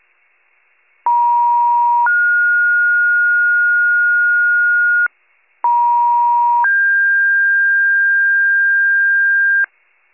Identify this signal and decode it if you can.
Two-tone page, heard as a one-second tone, then a three-second tone, about 950 then 1430 Hz, about 950 then 1600 Hz, with noise. It is heard from 1.1 to 9.6 s.